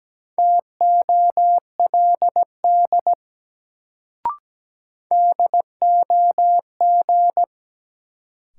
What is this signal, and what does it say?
Morse code, heard as a keyed tone, about 17 wpm, 700 Hz. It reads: TOLD DOG
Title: Sentences from Top 400 Words - New Word Drill - 17wpm